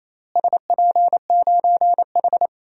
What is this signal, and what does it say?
Morse code, heard as a keyed tone, about 28 wpm, 700 Hz. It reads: SP9H
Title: DXCC-100 Call Signs - 4 Characters - Repeated 3x - 28wpm